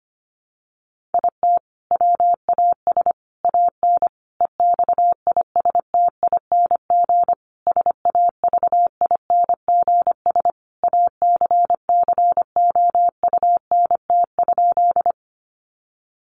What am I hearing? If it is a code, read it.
Morse code, heard as a keyed tone, about 25 wpm, 700 Hz. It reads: IT WAS AN EXISTING SAVINGS ACCOUNT?
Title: Sentences from Top 1500 Words 25wpm